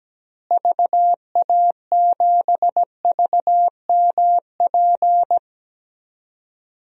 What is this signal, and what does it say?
Morse code, heard as a keyed tone, about 17 wpm, 700 Hz. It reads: VA7VMP
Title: DXCC-100 Call Signs - 6 Characters - Repeated 2x - 17wpm